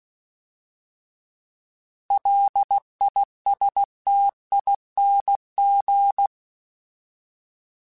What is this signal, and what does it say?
Morse code, heard as a keyed tone, about 16 wpm, 800 Hz. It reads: LISTING
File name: Alphabetical / Alphabetical mp3s / Ls @15